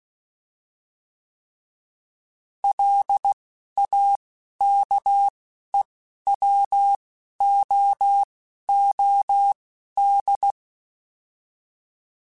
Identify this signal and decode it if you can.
Morse code, heard as a keyed tone, about 16 wpm, spaced as for 11 wpm, 800 Hz. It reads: LAKEWOOD
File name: fromSMs / FromSMs mp3s / Cities@10